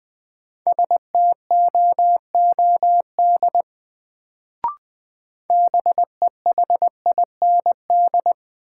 Morse code, heard as a keyed tone, about 20 wpm, 700 Hz. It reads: STOOD BEHIND
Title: Sentences from Top 500 Words - New Word Drill - 20wpm